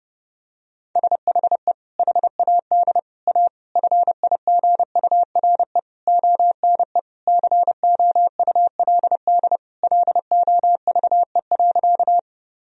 Morse code, heard as a keyed tone, about 30 wpm, 700 Hz. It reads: SHE HAD A FIGURE ONE COULD LOVE.
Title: Sentences from Top 1000 Words - Review Part 2 - 30wpm